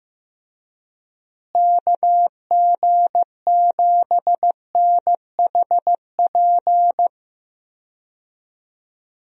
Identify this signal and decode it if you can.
Morse code, heard as a keyed tone, about 16 wpm, 700 Hz. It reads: KG7NHP
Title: USA Call Signs - 6 Characters - Repeated 3x - 15wpm